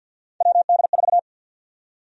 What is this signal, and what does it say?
Morse code, heard as a keyed tone, about 52 wpm, 700 Hz. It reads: WD4
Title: USA Call Sign Prefixes - Repeated 2x - 50wpm